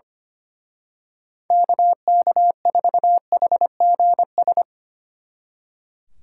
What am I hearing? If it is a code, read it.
Morse code, heard as a keyed tone, about 27 wpm, 700 Hz. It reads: KK4HGS
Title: Speed Racing - USA 2x3 Call Signs 25wpm